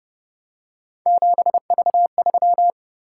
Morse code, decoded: 7V3

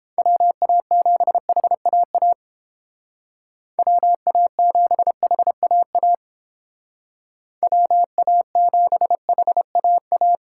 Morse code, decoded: WA7HAA WA7HAA WA7HAA

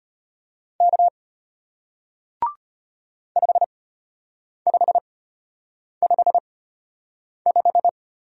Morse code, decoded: K 5555

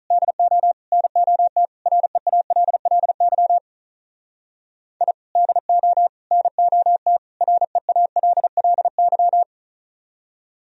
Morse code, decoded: DO NOT REALLY I DO NOT REALLY

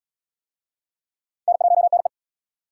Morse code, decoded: N1NE

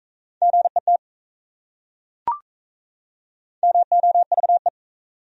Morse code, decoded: GET MOVE

